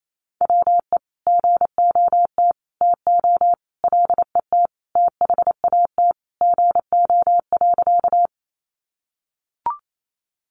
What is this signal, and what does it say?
Morse code, heard as a keyed tone, about 28 wpm, 700 Hz. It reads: WE GOT TO LET THAT GO.
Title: Sentences from Top 300 Words 28wpm